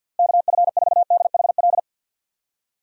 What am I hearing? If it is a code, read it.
Morse code, heard as a keyed tone, about 53 wpm, 700 Hz. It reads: KF4DHL
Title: USA Call Signs - 6 Characters - Repeated 2x - 50wpm